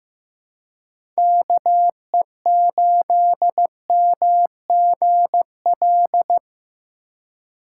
Morse code, heard as a keyed tone, about 15 wpm, 700 Hz. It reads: KE8MGL
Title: USA Call Signs - 6 Characters - 4x Character Spacing - 15wpm